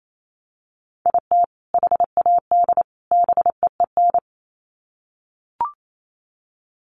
Morse code, heard as a keyed tone, about 28 wpm, 700 Hz. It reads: IT HAD BEEN
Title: Sets of 3 Words 28wpm